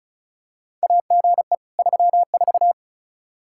Morse code, heard as a keyed tone, about 36 wpm, 700 Hz. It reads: AGE 34
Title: Practice QSO - Exchange 3 - More Info - 35wpm